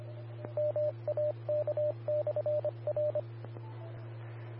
Morse code, heard as a keyed tone, about 25 wpm, 600 Hz, with radio noise. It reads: MAK/R
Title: Austin, TX Amateur Repeaters